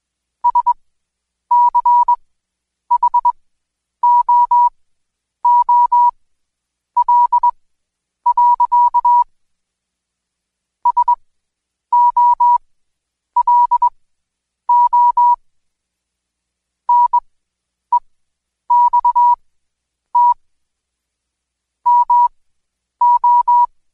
Morse code, heard as a keyed tone, about 20 wpm, spaced as for 9 wpm, 970 Hz. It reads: SCHOOL. SOLO NEXT MO
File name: Code Quick / AudioDownld2 / Seg116-Day21-DayDrill 21